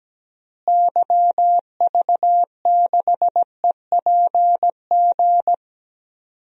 Morse code, heard as a keyed tone, about 17 wpm, 700 Hz. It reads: YV6EPG
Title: DXCC-100 Call Signs - 6 Characters - Repeated 2x - 17wpm